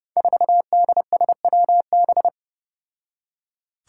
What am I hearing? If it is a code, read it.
Morse code, heard as a keyed tone, about 30 wpm, 700 Hz. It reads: 4DSWB
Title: Five Challenging Letter-Number Combinations 30wpm